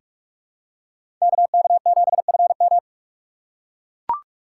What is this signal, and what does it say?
Morse code, heard as a keyed tone, about 42 wpm, 700 Hz. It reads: KK7FM
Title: USA Call Signs - 5 Characters - 6x Character Spacing - 45wpm